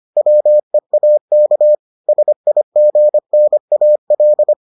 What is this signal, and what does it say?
Morse code, heard as a keyed tone, about 25 wpm, 600 Hz. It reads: WEAK SIGNAL